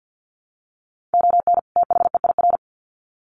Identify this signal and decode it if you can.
Morse code, heard as a keyed tone, about 50 wpm, 700 Hz. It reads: ON THEIR
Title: Sets of 2 Words 50wpm